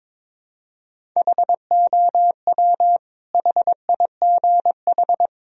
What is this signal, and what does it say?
Morse code, heard as a keyed tone, about 22 wpm, 700 Hz. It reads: HOW HIGH
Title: Sentences from Top 1000 Words - Review Part 1 - 22wpm